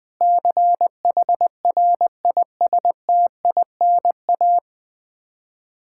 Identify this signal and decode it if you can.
Morse code, heard as a keyed tone, about 20 wpm, 700 Hz. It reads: CHRISTINA
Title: K1USN SST QSO Practice 20wpm